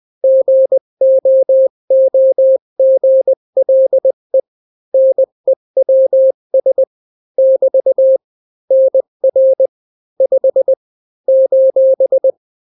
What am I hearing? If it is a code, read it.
Morse code, heard as a keyed tone, about 20 wpm, 530 Hz. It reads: GOOGLE NEWS <BT> NR 5 :
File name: daily / 18wpm